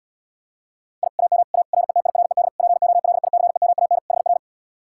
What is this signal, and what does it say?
Morse code, heard as a keyed tone, about 80 wpm, 700 Hz. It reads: I AM A RETIRED POLICEMAN HR